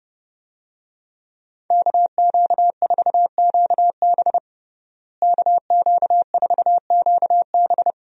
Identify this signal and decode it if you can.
Morse code, heard as a keyed tone, about 30 wpm, 700 Hz. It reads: KQ4QB KQ4QB
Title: USA Call Signs - 5 Characters - Repeated 2x - 30wpm